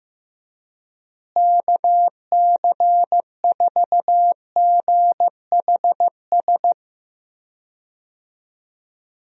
Morse code, heard as a keyed tone, about 15 wpm, 700 Hz. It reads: KC4GHS
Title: USA Call Signs - 6 Characters - Repeated 2x - 15wpm